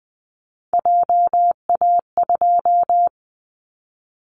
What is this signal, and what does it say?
Morse code, heard as a keyed tone, about 20 wpm, 700 Hz. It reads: JA2
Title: Callsign Prefixes for Top 50 DXCC - 20wpm